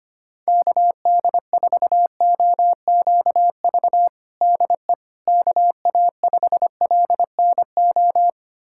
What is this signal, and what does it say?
Morse code, heard as a keyed tone, about 25 wpm, 700 Hz. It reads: KD4OQV DE KA5LNO
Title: USA Callsign DE Callsign - 6 Characters - 25wpm